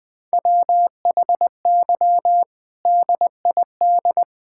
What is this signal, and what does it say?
Morse code, heard as a keyed tone, about 20 wpm, 700 Hz. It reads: WHY DID
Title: Sentences from Top 1000 Words - Review Part 6 - 20wpm